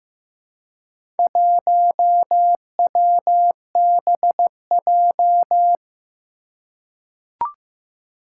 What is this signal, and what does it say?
Morse code, heard as a keyed tone, about 15 wpm, 700 Hz. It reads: 1WBJ
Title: Four Challenging Letter-Number Combinations 15wpm